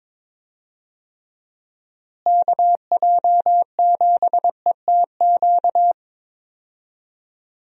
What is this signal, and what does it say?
Morse code, heard as a keyed tone, about 22 wpm, 700 Hz. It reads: KJ7ETQ
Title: USA Call Signs - 6 Characters - Repeated 2x - 22wpm